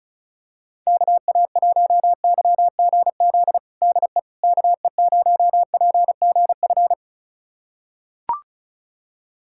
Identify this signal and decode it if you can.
Morse code, heard as a keyed tone, about 36 wpm, 700 Hz. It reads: KA1YGZ DE KE0PGF
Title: USA Callsign DE Callsign - 6 Characters - 35wpm